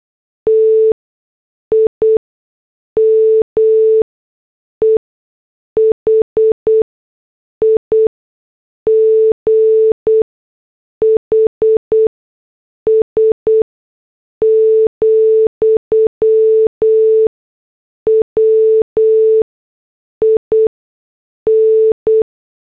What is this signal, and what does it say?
Morse code, heard as a keyed tone, about 8 wpm, spaced as for 6 wpm, 440 Hz. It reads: TIME HIGHS, WIN